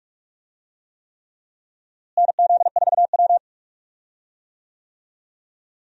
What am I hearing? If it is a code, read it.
Morse code, heard as a keyed tone, about 41 wpm, 700 Hz. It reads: NZ4W
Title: USA Call Signs - 4 Characters - Repeated 3x - 45wpm